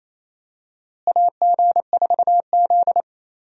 Morse code, heard as a keyed tone, about 28 wpm, 700 Hz. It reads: AG4Z
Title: USA Call Signs - 4 Characters - Repeated 2x - 28wpm